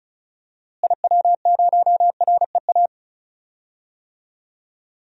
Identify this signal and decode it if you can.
Morse code, heard as a keyed tone, about 36 wpm, 700 Hz. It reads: IW0REA
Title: DXCC-100 Call Signs - 6 Characters - Repeated 2x - 35wpm